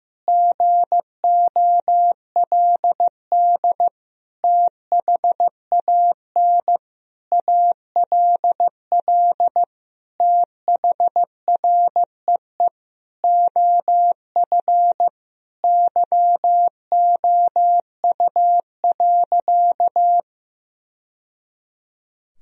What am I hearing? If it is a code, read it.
Morse code, heard as a keyed tone, about 15 wpm, 700 Hz. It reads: GOLD THAN ALL THREE OF YOU.
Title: Sentences from Top 1000 Words - Review Part 3 - 15wpm